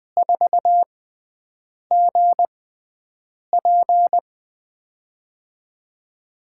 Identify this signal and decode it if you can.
Morse code, heard as a keyed tone, about 20 wpm, spaced as for 7 wpm, 700 Hz. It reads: 4GP
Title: USA Call Signs - 5 Characters - 6x Character Spacing - 20wpm